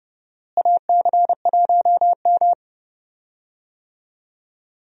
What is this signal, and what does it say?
Morse code, heard as a keyed tone, about 32 wpm, 700 Hz. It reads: AC1M
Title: USA Call Signs - 4 Characters - Repeated 3x - 30wpm